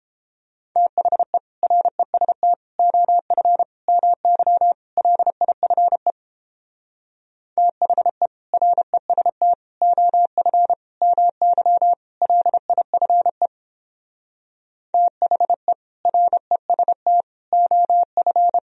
Morse code, heard as a keyed tone, about 29 wpm, 700 Hz. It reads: THE REST OF MY LIFE THE REST OF MY LIFE THE REST OF